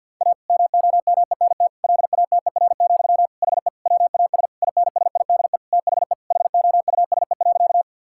Morse code, heard as a keyed tone, about 50 wpm, 700 Hz. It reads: A MOMENT LATER, HE WAS INSIDE THE HOUSE.
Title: Sentences from Top 1000 Words - Review Part 6 - 50wpm